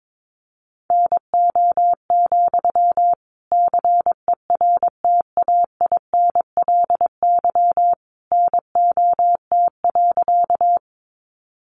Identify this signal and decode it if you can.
Morse code, heard as a keyed tone, about 22 wpm, 700 Hz. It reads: NO, CERTAINLY NOT.
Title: Sentences from Top 900 Words 22wpm